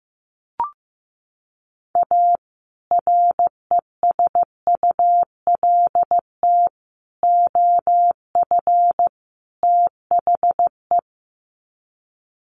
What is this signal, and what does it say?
Morse code, heard as a keyed tone, about 15 wpm, 700 Hz. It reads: A RESULT OF THE
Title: Sets of 4 Words 15wpm